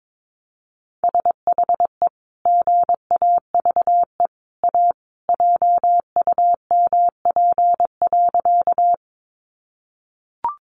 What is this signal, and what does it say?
Morse code, heard as a keyed tone, about 22 wpm, 700 Hz. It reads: SHE GAVE A JUMP.
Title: Sentences from Top 1300 Words 22wpm